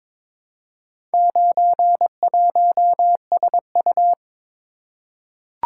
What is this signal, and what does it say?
Morse code, heard as a keyed tone, about 22 wpm, 700 Hz. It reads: 91SU
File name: practice / 22wpm / Four Letter-Number Combinations - 22wpm